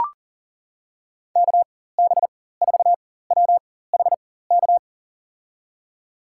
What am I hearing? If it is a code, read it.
Morse code, heard as a keyed tone, about 40 wpm, spaced as for 19 wpm, 700 Hz. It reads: KB4WHK
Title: USA Call Signs - 6 Characters - 4x Character Spacing - 40wpm